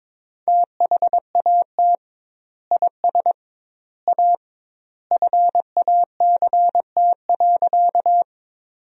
Morse code, heard as a keyed tone, about 21 wpm, 700 Hz. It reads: THAT IS A FACT.
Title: 2x Word-Spacing - Sentences from Top 500 Words - 22wpm